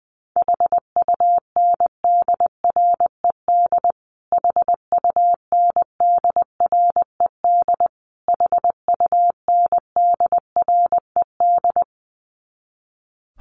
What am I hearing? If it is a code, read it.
Morse code, heard as a keyed tone, about 19 wpm, 700 Hz. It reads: HUNDRED HUNDRED HUNDRED
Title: Top 500 Words Repeated 20wpm